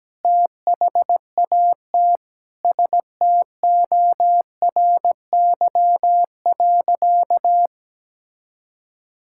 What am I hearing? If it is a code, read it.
Morse code, heard as a keyed tone, about 17 wpm, 700 Hz. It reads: THAT STORY.
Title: Sentences from Top 1000 Words - Review Part 5 - 17wpm